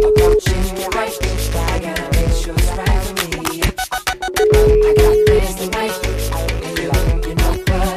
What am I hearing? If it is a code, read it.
Ringback tone (Europe, 425 Hz), with noise.